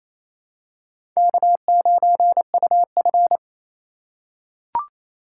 Morse code, heard as a keyed tone, about 28 wpm, 700 Hz. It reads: K9UF
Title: USA Call Signs - 4 Characters - 4x Character Spacing - 28wpm